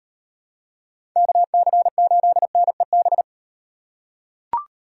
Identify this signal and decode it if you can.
Morse code, heard as a keyed tone, about 38 wpm, 700 Hz. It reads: KC8NEB